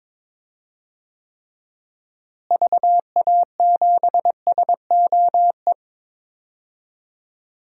Morse code, heard as a keyed tone, about 22 wpm, 700 Hz. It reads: VA7SOE
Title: DXCC-100 Call Signs - 6 Characters - Repeated 2x - 22wpm